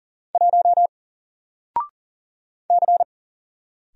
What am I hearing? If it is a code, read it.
Morse code, heard as a keyed tone, about 38 wpm, 700 Hz. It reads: 1 C
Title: Single Letter-Number Alt 40wpm